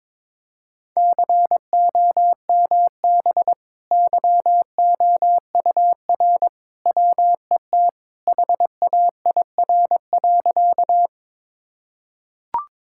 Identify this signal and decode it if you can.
Morse code, heard as a keyed tone, about 22 wpm, 700 Hz. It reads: COMB YOUR WET HAIR.
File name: practice / 22wpm / Easy Sentences with Hints - 22wpm